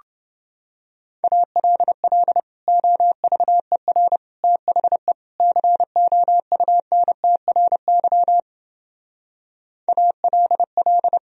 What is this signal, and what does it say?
Morse code, heard as a keyed tone, about 30 wpm, 700 Hz. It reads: ALL OVER THE COUNTRY ALL